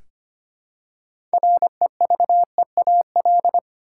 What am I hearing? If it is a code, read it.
Morse code, heard as a keyed tone, about 25 wpm, 700 Hz. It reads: REVEAL
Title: Six Letter Words - 25wpm